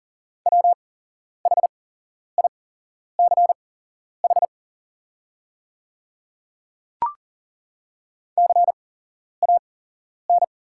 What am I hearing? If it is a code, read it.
Morse code, heard as a keyed tone, about 40 wpm, spaced as for 11 wpm, 700 Hz. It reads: WHICH CAN
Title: 8x Character Spacing - Top 100 Words - 40wpm